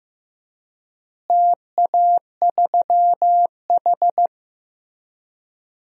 Morse code, heard as a keyed tone, about 15 wpm, 700 Hz. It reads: TA3H
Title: DXCC-100 Call Signs - 4 Characters - Repeated 3x - 15wpm